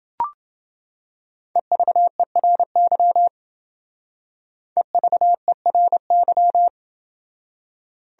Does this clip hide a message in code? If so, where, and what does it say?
Yes: Morse code from 1.6 to 6.7 s, heard as a keyed tone, about 28 wpm, 700 Hz. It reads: EVERY EVERY